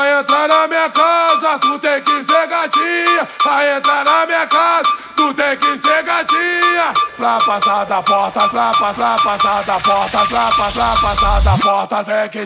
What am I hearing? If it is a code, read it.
There is no code.